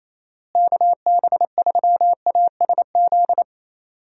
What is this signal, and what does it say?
Morse code, heard as a keyed tone, about 28 wpm, 700 Hz. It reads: KB3ASZ
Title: USA Call Signs - 6 Characters - Repeated 2x - 28wpm